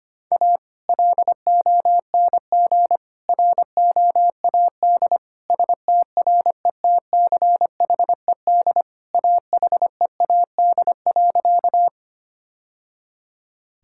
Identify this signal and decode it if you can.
Morse code, heard as a keyed tone, about 25 wpm, 700 Hz. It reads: A LONG ROAD STRETCHED AHEAD.
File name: practice / 25wpm / Sentences with 2-Word Phrases - 25wpm